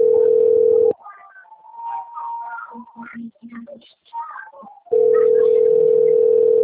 Ringback tone (North America), 440 + 480 Hz.
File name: out- - - - - .2442